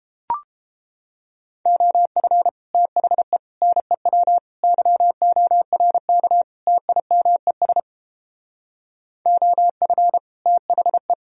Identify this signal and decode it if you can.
Morse code, heard as a keyed tone, about 32 wpm, 700 Hz. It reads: OF THE NEW YORK TIMES OF THE